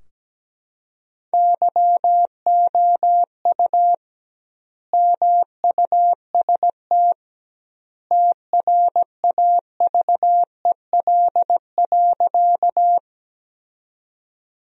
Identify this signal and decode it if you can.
Morse code, heard as a keyed tone, about 17 wpm, 700 Hz. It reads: YOU MUST TRAVEL.
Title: 2x Word-Spacing - Sentences from Top 400 Words - 17wpm